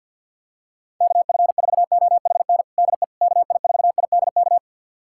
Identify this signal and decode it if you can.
Morse code, heard as a keyed tone, about 48 wpm, 700 Hz. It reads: KF4OHN DE KI4IDK
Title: USA Callsign DE Callsign - 6 Characters - 50wpm